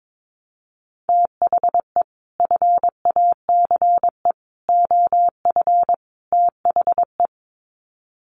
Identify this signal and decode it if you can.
Morse code, heard as a keyed tone, about 22 wpm, 700 Hz. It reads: THE FACE OF THE
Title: Sets of 4 Words 22wpm